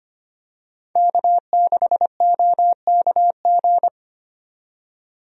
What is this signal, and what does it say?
Morse code, heard as a keyed tone, about 25 wpm, 700 Hz. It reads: K6OKG
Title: USA Call Signs - 5 Characters - 25wpm